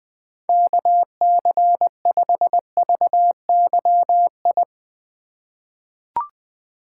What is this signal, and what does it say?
Morse code, heard as a keyed tone, about 20 wpm, 700 Hz. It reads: KC5VYI
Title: USA Call Signs - 6 Characters - 4x Character Spacing - 20wpm